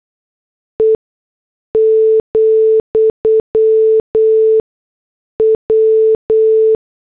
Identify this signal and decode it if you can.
Morse code, heard as a keyed tone, about 8 wpm, spaced as for 6 wpm, 440 Hz. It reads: E, W
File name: morse / headlines